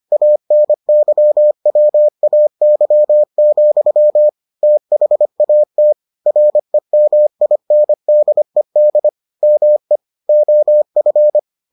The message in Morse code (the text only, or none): ANYWAY, THAT REMINDED ME OF